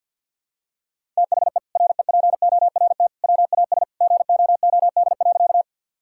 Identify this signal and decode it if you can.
Morse code, heard as a keyed tone, about 49 wpm, 700 Hz. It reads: THE REPORT WAS GOOD.
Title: Sentences from Top 1000 Words - Review Part 6 - 50wpm